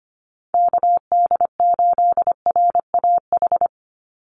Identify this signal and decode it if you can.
Morse code, heard as a keyed tone, about 25 wpm, 700 Hz. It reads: KD8RAH
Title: USA Call Signs 25wpm